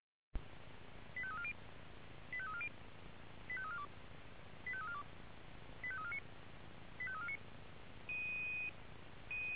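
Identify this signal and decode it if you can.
Five-tone selective call (ZVEI), heard as quick runs of five tones, tones of about 70 ms. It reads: 85340 85340 85342 85342 85349 85349
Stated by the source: ZVEI five-tone selective call received on SDR.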